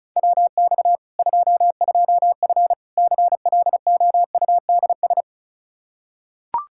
Morse code, heard as a keyed tone, about 36 wpm, 700 Hz. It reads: WX 22F CLOUDS
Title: Practice QSO - Exchange 2 - Info - 35wpm